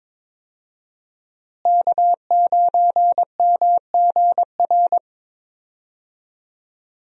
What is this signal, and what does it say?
Morse code, heard as a keyed tone, about 22 wpm, 700 Hz. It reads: K9MGR
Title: USA Call Signs - 5 Characters - Repeated 3x - 22wpm